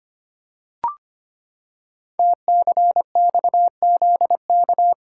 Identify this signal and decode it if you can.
Morse code, heard as a keyed tone, about 25 wpm, 700 Hz. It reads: TCXZK
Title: Code Groups - 5 Letters - 25wpm